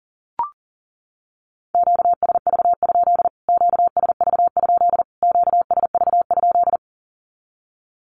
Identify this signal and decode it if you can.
Morse code, heard as a keyed tone, about 40 wpm, 700 Hz. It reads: QSV? QSV? QSV?
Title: Q-Codes Repeated 3x 40wpm